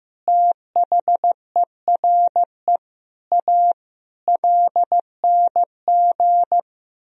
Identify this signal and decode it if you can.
Morse code, heard as a keyed tone, about 15 wpm, 700 Hz. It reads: THERE A LNG
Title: Sentences from Top 400 Words and Abbreviations 15wpm